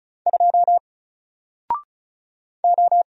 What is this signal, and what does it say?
Morse code, heard as a keyed tone, about 34 wpm, 700 Hz. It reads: 2 O
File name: Single-Letter-Number-35wpm